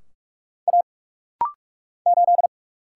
Morse code, heard as a keyed tone, about 44 wpm, 700 Hz. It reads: A 8
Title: Single Letter-Number - Rapid-Fire 45wpm